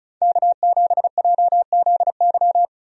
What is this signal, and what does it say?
Morse code, heard as a keyed tone, about 36 wpm, 700 Hz. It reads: K7JZY
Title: USA Call Signs - 5 Characters - 35wpm